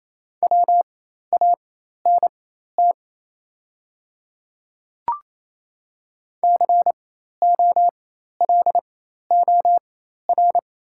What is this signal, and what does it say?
Morse code, heard as a keyed tone, about 28 wpm, spaced as for 13 wpm, 700 Hz. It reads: WANT COLOR
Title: 4x Character Spacing - Top 500 Words - 28wpm